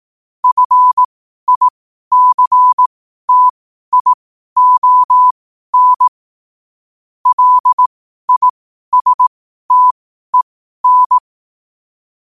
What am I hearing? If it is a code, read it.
Morse code, heard as a keyed tone, about 18 wpm, spaced as for 12 wpm, 1000 Hz. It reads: FICTION LISTEN
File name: qotdfiles / 12wpm